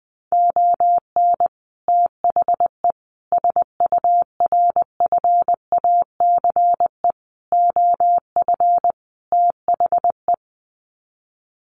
Morse code, heard as a keyed tone, about 20 wpm, 700 Hz. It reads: ON THE SURFACE OF THE
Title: Sets of 5 Words Encore 20wpm